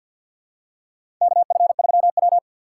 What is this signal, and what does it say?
Morse code, heard as a keyed tone, about 50 wpm, 700 Hz. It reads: KF3W
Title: USA Call Signs - 4 Characters - 3x Character Spacing - 50wpm